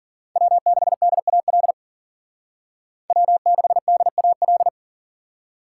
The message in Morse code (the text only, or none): W6DAL W6DAL